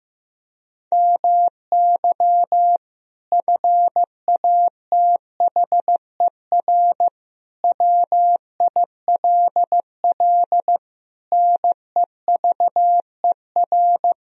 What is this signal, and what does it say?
Morse code, heard as a keyed tone, about 15 wpm, 700 Hz. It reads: MY FATHER WILL NEVER
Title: Sentences from Top 1000 Words - Review Part 4 - 15wpm